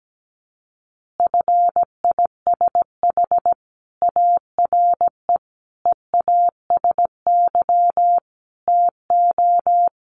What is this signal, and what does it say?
Morse code, heard as a keyed tone, about 17 wpm, 700 Hz. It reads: FISH ARE EASY TO
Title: Easy Phrases 17wpm